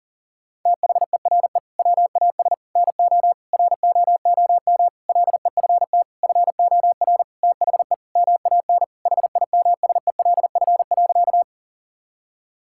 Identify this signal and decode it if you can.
Morse code, heard as a keyed tone, about 40 wpm, 700 Hz. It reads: THERE WAS NO ROOM LEFT FOR THE MAN HIMSELF.
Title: Sentences from Top 1600 Words 40wpm